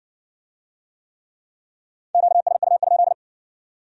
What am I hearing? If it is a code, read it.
Morse code, heard as a keyed tone, about 56 wpm, 700 Hz. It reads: QSU?